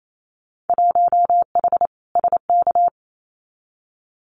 Morse code, heard as a keyed tone, about 28 wpm, 700 Hz. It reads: 1H SK
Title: Winter Field Day Exchanges 28wpm